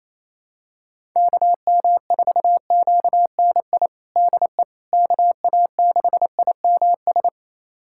Morse code, heard as a keyed tone, about 28 wpm, 700 Hz. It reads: KM4QNI DE KA6IMS
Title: USA Callsign DE Callsign - 6 Characters - 28wpm